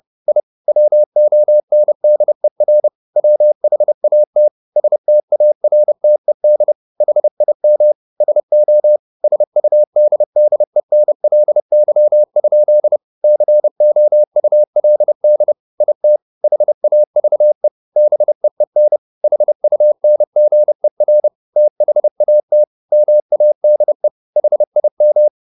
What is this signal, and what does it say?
Morse code, heard as a keyed tone, about 30 wpm, 600 Hz. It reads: I WONDER WHAT STARTED HIM SO SUDDENLY? COULD IT HAVE BEEN HUNGER THAT MADE HIM